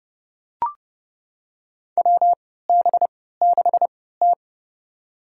Morse code, heard as a keyed tone, about 30 wpm, spaced as for 17 wpm, 700 Hz. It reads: WB6T
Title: USA Call Signs - 4 Characters - 3x Character Spacing - 30wpm